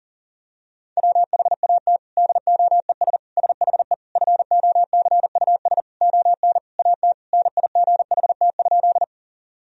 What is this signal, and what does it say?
Morse code, heard as a keyed tone, about 40 wpm, 700 Hz. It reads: WHAT DOES SHE FOCUS ON AT NIGHT?
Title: Sentences from Top 1000 Words - Review Part 4 - 40wpm